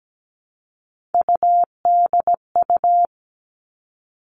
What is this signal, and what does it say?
Morse code, heard as a keyed tone, about 17 wpm, 700 Hz. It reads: UDU
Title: Three Challenging Letter-Number Combinations 17wpm